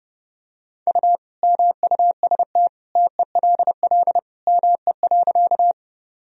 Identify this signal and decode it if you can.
Morse code, heard as a keyed tone, about 32 wpm, 700 Hz. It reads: U MUST TELL ME.
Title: Sentences from Top 200 Words and Abbreviations 30wpm